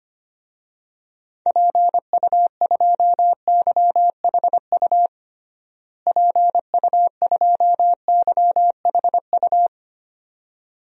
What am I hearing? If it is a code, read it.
Morse code, heard as a keyed tone, about 25 wpm, 700 Hz. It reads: PU2YHU PU2YHU
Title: DXCC-100 Call Signs - 6 Characters - Repeated 2x - 25wpm